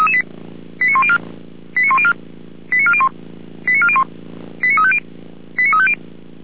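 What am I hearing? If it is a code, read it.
Five-tone selective call (ZVEI), heard as quick runs of five tones, tones of about 70 ms. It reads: 89104 89104 89401 89401 89350 89350
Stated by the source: ZVEI five-tone selective call 89164 received on SDR, logged by Feuerwehr Orlamünde.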